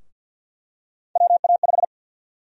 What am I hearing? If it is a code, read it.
Morse code, heard as a keyed tone, about 50 wpm, 700 Hz. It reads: WA5
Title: USA Call Sign Prefixes - 50wpm